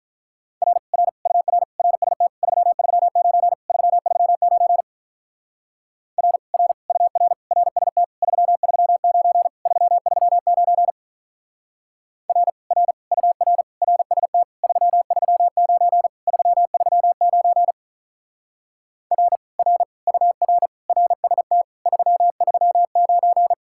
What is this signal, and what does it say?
Morse code, heard as a keyed tone, about 46 wpm, 700 Hz. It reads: R R UR RST 339 339 R R UR RST 339 339 R R UR RST 339 339 R R UR RST 339